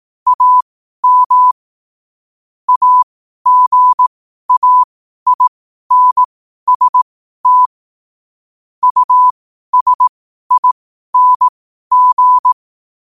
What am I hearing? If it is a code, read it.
Morse code, heard as a keyed tone, about 18 wpm, spaced as for 12 wpm, 1000 Hz. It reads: AM AGAINST USING
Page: New quotes every day in morse code at 12 Words per minute.